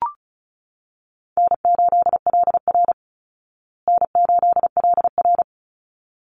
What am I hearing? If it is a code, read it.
Morse code, heard as a keyed tone, about 34 wpm, 700 Hz. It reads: N8LR N8LR
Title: Easy Call Signs Repeated 2x - 35wpm